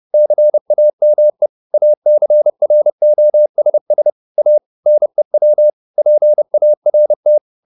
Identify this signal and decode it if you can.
Morse code, heard as a keyed tone, about 30 wpm, 600 Hz. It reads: CAME ACROSS A NEW PART
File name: the old man and the sea / the old man and the sea 1 30wpm